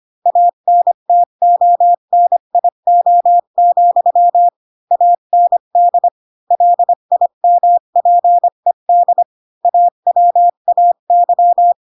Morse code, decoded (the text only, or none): ANTONIO, AND LIMPED AWAY